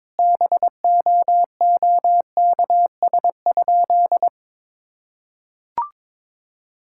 Morse code, heard as a keyed tone, about 21 wpm, 700 Hz. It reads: BOOKS?
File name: Sentences with 9-Word Phrases - 22wpm